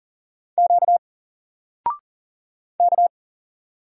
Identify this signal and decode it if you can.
Morse code, heard as a keyed tone, about 40 wpm, 700 Hz. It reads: Q K
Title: Single Character (Letter:Number:Punctuation:Prosign) 40wpm